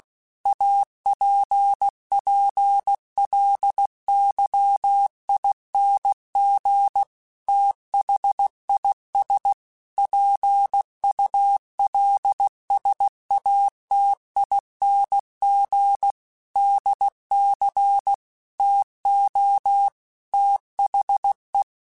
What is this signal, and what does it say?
Morse code, heard as a keyed tone, about 16 wpm, 800 Hz. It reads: APPLYING THIS PULSATING DC TO THE
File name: Technical Subjects / Technical Subjects-mp3s / Sw Pwr Sup @15